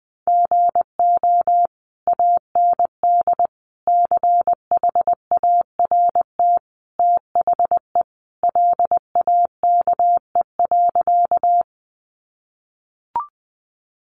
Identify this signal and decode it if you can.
Morse code, heard as a keyed tone, about 20 wpm, 700 Hz. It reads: GO AND CHART THE LAKE.
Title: Sentences from Top 1300 Words 20wpm